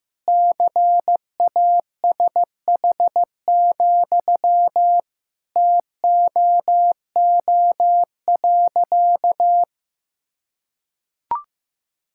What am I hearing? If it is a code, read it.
Morse code, heard as a keyed tone, about 15 wpm, 700 Hz. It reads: CASH, TOO.
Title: Sentences from Top 1000 Words - Review Part 5 - 15wpm